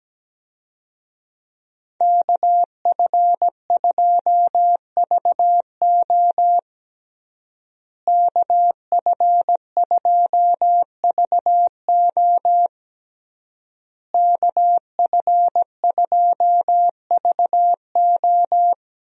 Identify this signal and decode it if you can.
Morse code, heard as a keyed tone, about 17 wpm, 700 Hz. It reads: KF2VO KF2VO KF2VO
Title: USA Call Signs - 5 Characters - Repeated 3x - 17wpm